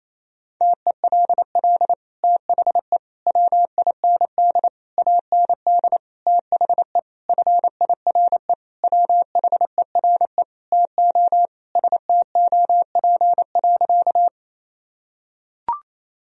Morse code, heard as a keyed tone, about 27 wpm, 700 Hz. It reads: TELL THE WIND AND THE FIRE WHERE TO STOP.
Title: Sentences from Top 1000 Words - Review Part 2 - 28wpm